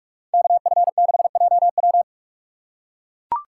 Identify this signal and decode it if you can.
Morse code, heard as a keyed tone, about 42 wpm, 700 Hz. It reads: KF6JW